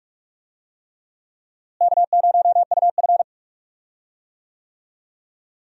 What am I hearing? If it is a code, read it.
Morse code, heard as a keyed tone, about 44 wpm, 700 Hz. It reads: K0UF
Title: USA Call Signs - 4 Characters - Repeated 3x - 45wpm